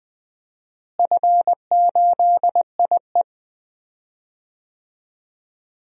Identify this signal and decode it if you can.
Morse code, heard as a keyed tone, about 20 wpm, 700 Hz. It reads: F8IE
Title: DXCC-100 Call Signs - 4 Characters - Repeated 2x - 20wpm